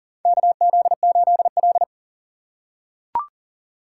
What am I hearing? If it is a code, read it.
Morse code, heard as a keyed tone, about 40 wpm, 700 Hz. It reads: KZ8L
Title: USA Call Signs - 4 Characters - 6x Character Spacing - 40wpm